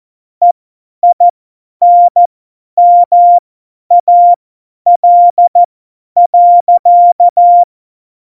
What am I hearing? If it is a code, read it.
Morse code, heard as a keyed tone, about 14 wpm, spaced as for 10 wpm, 700 Hz. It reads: EINMAL.
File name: Morsekurs / Witze / Witz 24. Oktober 17, Tempo 50